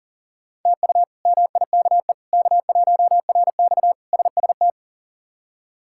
Morse code, heard as a keyed tone, about 40 wpm, 700 Hz. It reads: TU MIKE K1RX SST
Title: K1USN SST QSO Practice - 40wpm